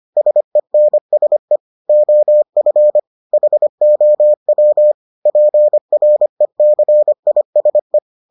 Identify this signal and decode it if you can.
Morse code, heard as a keyed tone, about 25 wpm, 600 Hz. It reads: SENSE OF HOW PRECISE